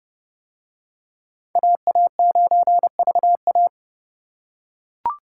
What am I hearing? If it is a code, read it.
Morse code, heard as a keyed tone, about 29 wpm, 700 Hz. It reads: AA9VA